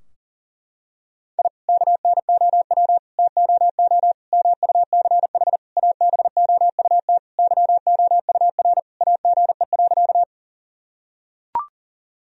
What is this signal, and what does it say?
Morse code, heard as a keyed tone, about 40 wpm, 700 Hz. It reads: I KNOW TOO MUCH ABOUT YOUR AGE.
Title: Sentences from Top 1000 Words - Review Part 3 - 40wpm